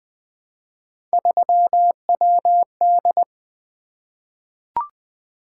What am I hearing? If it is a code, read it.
Morse code, heard as a keyed tone, about 20 wpm, 700 Hz. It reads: 3WD
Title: Three Letter-Number Combinations - 20wpm